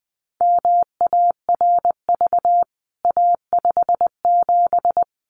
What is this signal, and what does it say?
Morse code, heard as a keyed tone, about 19 wpm, 700 Hz. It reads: MARV A57
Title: CWT Exchange 20wpm